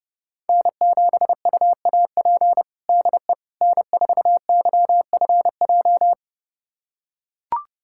Morse code, heard as a keyed tone, about 30 wpm, 700 Hz. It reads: N7UAP DE N4YFJ
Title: USA Callsign DE Callsign 30wpm